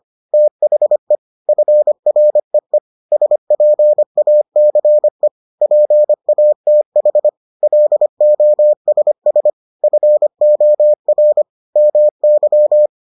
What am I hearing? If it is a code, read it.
Morse code, heard as a keyed tone, about 25 wpm, 600 Hz. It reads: THE FREE SPACE PATH LOSS FOR MY